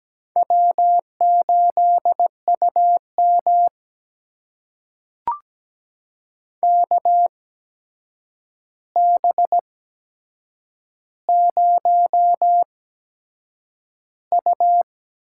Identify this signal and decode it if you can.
Morse code, heard as a keyed tone, about 17 wpm, spaced as for 7 wpm, 700 Hz. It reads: W8UM KB0U
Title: USA Call Signs - 4 Characters - 8x Character Spacing - 17wpm